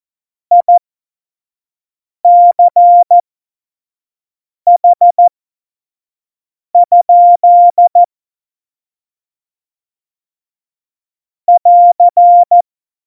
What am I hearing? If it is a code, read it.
Morse code, heard as a keyed tone, about 14 wpm, spaced as for 5 wpm, 700 Hz. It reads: ICH? <AR>
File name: Witz 31. Oktober 17, Tempo 25